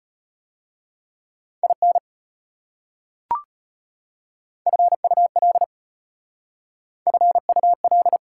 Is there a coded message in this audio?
Yes: Morse code from 1.6 to 8.2 s, heard as a keyed tone, about 33 wpm, 700 Hz. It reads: IN FUL FUL